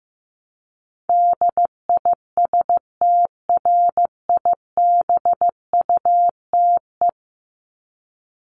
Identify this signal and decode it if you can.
Morse code, heard as a keyed tone, about 15 wpm, 700 Hz. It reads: DISTRIBUTE
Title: Top 1501-1750 Words 15wpm